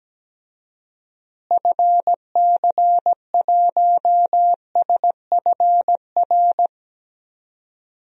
Morse code, heard as a keyed tone, about 17 wpm, 700 Hz. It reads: FC1SFR
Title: DXCC-100 Call Signs - 6 Characters - Repeated 3x - 17wpm